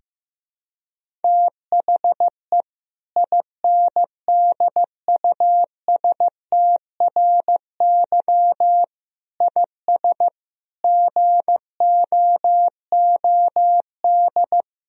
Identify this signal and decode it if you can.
Morse code, heard as a keyed tone, about 15 wpm, 700 Hz. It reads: THE INDUSTRY IS GOOD
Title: Sentences from Top 1000 Words - Review Part 4 - 15wpm